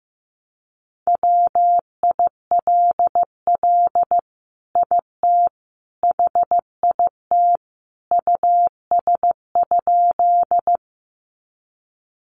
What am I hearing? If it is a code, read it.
Morse code, heard as a keyed tone, about 15 wpm, 700 Hz. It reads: WILL IT HIT US?
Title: Sentences from Top 1200 Words 15wpm